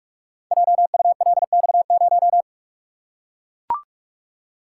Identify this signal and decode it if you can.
Morse code, heard as a keyed tone, about 44 wpm, 700 Hz. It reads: JULX0
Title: Code Groups - 5 Letter:Numbers - 45wpm